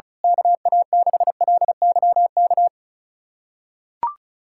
Morse code, heard as a keyed tone, about 36 wpm, 700 Hz. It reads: KA6LYK
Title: USA Callsign DE Callsign 35wpm